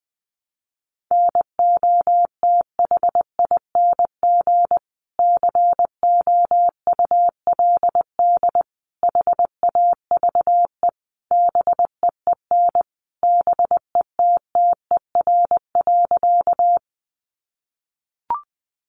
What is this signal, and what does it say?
Morse code, heard as a keyed tone, about 20 wpm, 700 Hz. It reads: NOTHING COULD HAVE BEEN BETTER.
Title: Sentences from Top 500 Words 20wpm